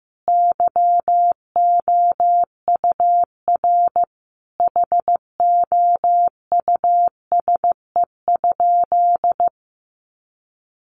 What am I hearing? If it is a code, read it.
Morse code, heard as a keyed tone, about 15 wpm, 700 Hz. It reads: YOUR HOUSE?
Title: Sentences from Top 1000 Words 15wpm